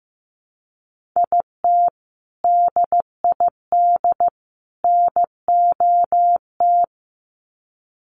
Morse code, heard as a keyed tone, about 15 wpm, 700 Hz. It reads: IT DID NOT
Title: Sets of 3 Words 15wpm